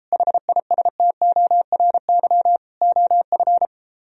Morse code, heard as a keyed tone, about 34 wpm, 700 Hz. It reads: HISTORY OF